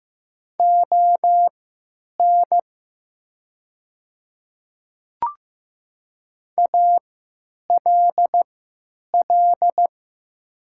Morse code, heard as a keyed tone, about 15 wpm, spaced as for 9 wpm, 700 Hz. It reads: ON ALL
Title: 3x Character Spacing - Top 100 Words - 15wpm